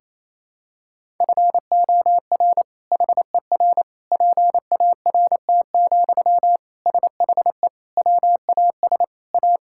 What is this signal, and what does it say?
Morse code, heard as a keyed tone, about 28 wpm, 700 Hz. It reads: FOR HER PART, SHE WAS A
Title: Sentences from Top 1000 Words - Review Part 5 - 28wpm